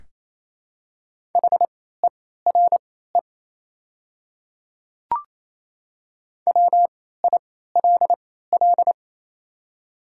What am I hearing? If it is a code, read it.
Morse code, heard as a keyed tone, about 28 wpm, spaced as for 16 wpm, 700 Hz. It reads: HERE WILL
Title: 3x Character Spacing - Top 500 Words - 28wpm